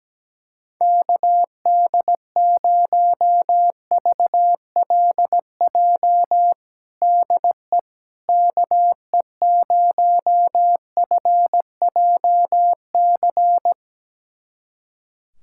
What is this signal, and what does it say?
Morse code, heard as a keyed tone, about 17 wpm, 700 Hz. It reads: KD0VLJ DE KE0FJC
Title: USA Callsign DE Callsign - 6 Characters - 17wpm